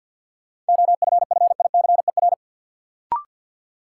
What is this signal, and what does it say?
Morse code, heard as a keyed tone, about 52 wpm, 700 Hz. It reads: OFFICER
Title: Sentences from Top 1200 Words - New Word Drill - 50wpm